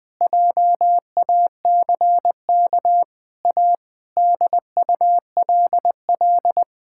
Morse code, heard as a keyed tone, about 20 wpm, 700 Hz. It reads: JACK A DULL
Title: Sentences with 7-Word Phrases - 20wpm